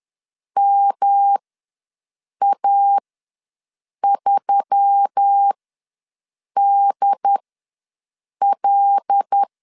Morse code, heard as a keyed tone, about 10 wpm, spaced as for 6 wpm, 800 Hz. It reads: MA3DL
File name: LEZIONI / 5 LBVC / 5 L ws 10wpm 3636